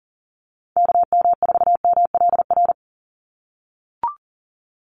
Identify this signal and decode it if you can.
Morse code, heard as a keyed tone, about 40 wpm, 700 Hz. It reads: KM4MLR
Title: USA Call Signs 40wpm